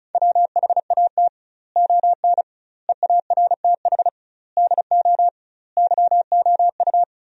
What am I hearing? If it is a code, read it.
Morse code, heard as a keyed tone, about 36 wpm, 700 Hz. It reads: WHAT ON EARTH DO YOU
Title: 2x Word-Spacing - Sentences from Top 200 Words - 35wpm